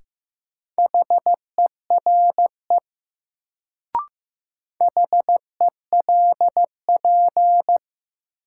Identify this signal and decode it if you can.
Morse code, heard as a keyed tone, about 15 wpm, 700 Hz. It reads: HERE HELP
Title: Sentences from Top 200 Words - New Word Drill - 15wpm